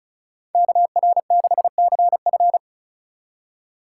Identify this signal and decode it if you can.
Morse code, heard as a keyed tone, about 44 wpm, 700 Hz. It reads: KR6CF
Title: Speed Racing - USA Call Signs 35wpm